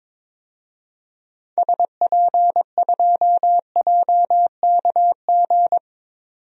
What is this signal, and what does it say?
Morse code, heard as a keyed tone, about 22 wpm, 700 Hz. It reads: SP2JKG
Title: DXCC-100 Call Signs - 6 Characters - 22wpm